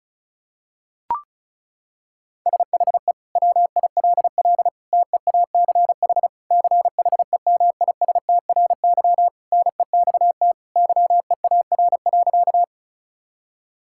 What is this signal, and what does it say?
Morse code, heard as a keyed tone, about 36 wpm, 700 Hz. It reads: SHE WILL TEACH CHEMISTRY NEXT YEAR.
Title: Sentences from Top 1000 Words - Review Part 6 - 35wpm